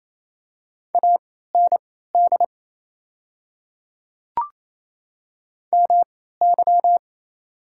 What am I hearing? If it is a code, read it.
Morse code, heard as a keyed tone, about 27 wpm, spaced as for 16 wpm, 700 Hz. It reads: AND MY
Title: 3x Character Spacing - Top 100 Words - 28wpm